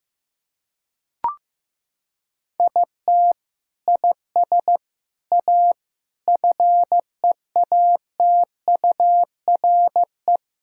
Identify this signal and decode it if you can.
Morse code, heard as a keyed tone, about 15 wpm, 700 Hz. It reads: IT IS A FEATURE
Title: Sentences from Top 1600 Words 15wpm